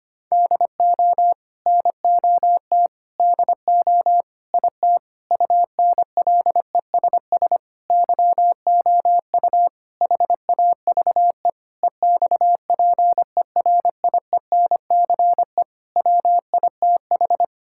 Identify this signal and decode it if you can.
Morse code, heard as a keyed tone, about 25 wpm, 700 Hz. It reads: DO NOT DO IT UNLESS YOU HAVE EXPERIENCE WITH
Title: Sentences from Top 1000 Words - Review Part 6 - 25wpm